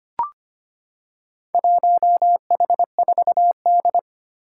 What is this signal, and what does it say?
Morse code, heard as a keyed tone, about 25 wpm, 700 Hz. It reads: 1H4D
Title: Four Challenging Letter-Number Combinations 25wpm